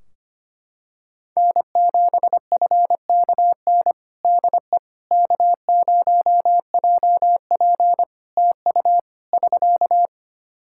Morse code, heard as a keyed tone, about 25 wpm, 700 Hz. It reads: N7FKN DE K0JP TU <SK>
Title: Practice QSO - Exchange 4 - Ending - 25wpm